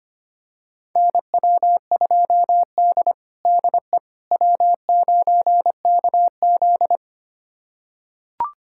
Morse code, heard as a keyed tone, about 25 wpm, 700 Hz. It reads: NW2D DE W9KZ
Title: USA Callsign DE Callsign - 4 Characters - 25wpm